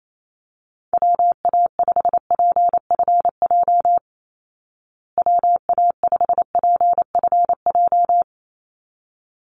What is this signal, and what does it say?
Morse code, heard as a keyed tone, about 28 wpm, 690 Hz. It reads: WA5PFJ WA5PFJ
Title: Call Signs Repeated 3x - 28wpm